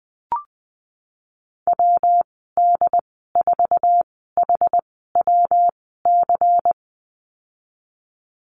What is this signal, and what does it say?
Morse code, heard as a keyed tone, about 20 wpm, spaced as for 15 wpm, 700 Hz. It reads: WD4HWC
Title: 122 Lesson 20g - Number 3 - Call Signs 20wpm